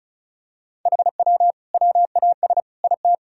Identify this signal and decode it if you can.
Morse code, heard as a keyed tone, about 34 wpm, 700 Hz. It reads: HW WAS IT
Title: Sentences from Top 500 Words and Abbreviations 35wpm